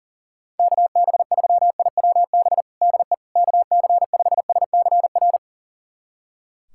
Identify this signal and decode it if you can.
Morse code, heard as a keyed tone, about 40 wpm, 700 Hz. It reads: KB3IWB DE KC5SCR
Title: USA Callsign DE Callsign - 6 Characters - 40wpm